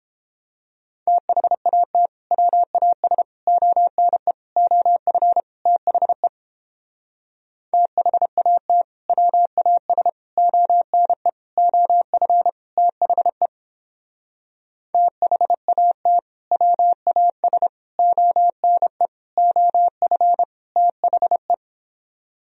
Morse code, heard as a keyed tone, about 29 wpm, 700 Hz. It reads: THAT WAS ONE OF THE THAT WAS ONE OF THE THAT WAS ONE OF THE